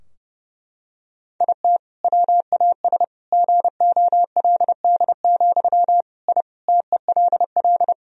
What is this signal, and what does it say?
Morse code, heard as a keyed tone, about 30 wpm, 700 Hz. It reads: IT WAS GOLD, I TELL
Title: Sentences from Top 1000 Words - Review Part 3 - 30wpm